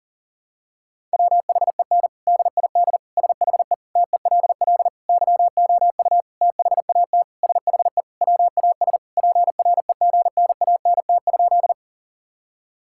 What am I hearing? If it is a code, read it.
Morse code, heard as a keyed tone, about 40 wpm, 700 Hz. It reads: WHEN DID SHE TELL YOU THAT SHE WAS PREGNANT?
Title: Sentences from Top 1000 Words - Review Part 6 - 40wpm